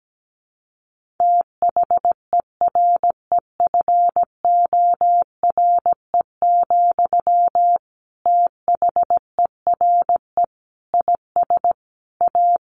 Morse code, heard as a keyed tone, about 17 wpm, 700 Hz. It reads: THEREFORE, THERE IS A
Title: Sentences from Top 1500 Words 17wpm